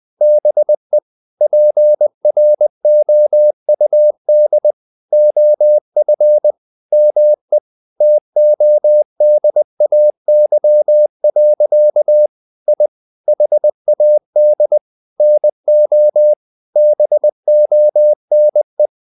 Morse code, heard as a keyed tone, about 20 wpm, 600 Hz. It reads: BE PROUD OF ME TODAY. I HAD NO BONE
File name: the old man and the sea part 2 20wpm